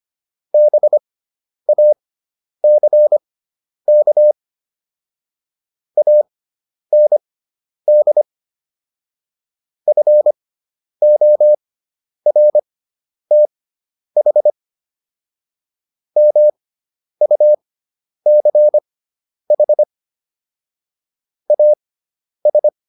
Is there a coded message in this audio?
Yes: Morse code from 0.5 to 22.7 s, heard as a keyed tone, about 25 wpm, spaced as for 10 wpm, 600 Hz. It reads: BACK AND FORTH MUCH AS